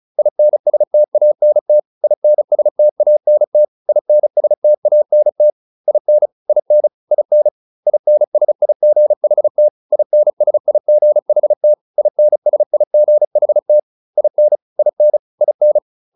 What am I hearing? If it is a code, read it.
Morse code, heard as a keyed tone, about 34 wpm, 600 Hz. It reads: INSTANT INSTANT INSTANT IN IN IN INSIGHT INSIGHT INSIGHT IN IN IN
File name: IN7 35